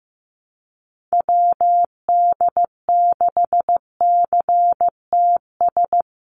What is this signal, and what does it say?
Morse code, heard as a keyed tone, about 15 wpm, 700 Hz. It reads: WD6CTS
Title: USA Call Signs Repeated 3x 15wpm